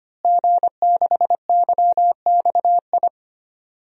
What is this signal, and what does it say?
Morse code, heard as a keyed tone, about 25 wpm, 700 Hz. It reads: G6YXI
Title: DXCC-100 Call Signs - 5 Characters - Repeated 3x - 25wpm